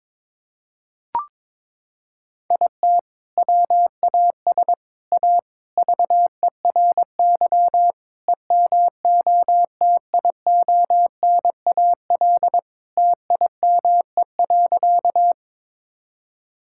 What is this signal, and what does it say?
Morse code, heard as a keyed tone, about 22 wpm, 700 Hz. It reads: IT WAS A VERY EMOTIONAL TIME.
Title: Sentences from Top 1000 Words - Review Part 5 - 22wpm